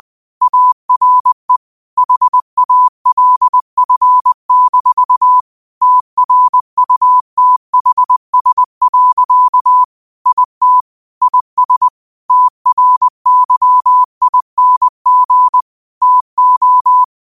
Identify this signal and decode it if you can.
Morse code, heard as a keyed tone, about 20 wpm, 1000 Hz. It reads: ARE HALF- TRUTHS. IT IS TRYING TO